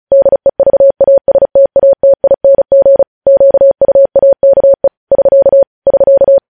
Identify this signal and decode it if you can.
Morse code, heard as a keyed tone, about 36 wpm, 550 Hz. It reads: DEVASTATING QUAKE <SK> <SK>
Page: News headlines in Morse code at 35 wpm